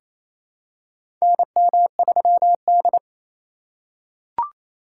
Morse code, heard as a keyed tone, about 28 wpm, 700 Hz. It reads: NM3D
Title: USA Call Signs - 4x Character Spacing - 28wpm